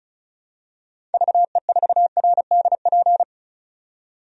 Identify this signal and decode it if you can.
Morse code, heard as a keyed tone, about 34 wpm, 700 Hz. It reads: VE4RDP
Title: DXCC-100 Call Signs - 6 Characters - 35wpm